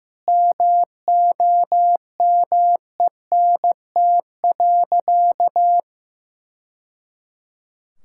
Morse code, heard as a keyed tone, about 16 wpm, 700 Hz. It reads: MOMENT.
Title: Sentences from Top 1000 Words - Review Part 6 - 15wpm